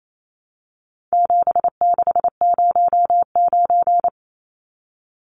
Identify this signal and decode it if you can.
Morse code, heard as a keyed tone, about 28 wpm, 700 Hz. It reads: 7609
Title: Numbers - 4-Digit 28wpm